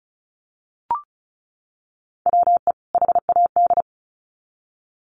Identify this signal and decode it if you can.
Morse code, heard as a keyed tone, about 34 wpm, 700 Hz. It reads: WE HAD
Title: Sets of 2 Words 35wpm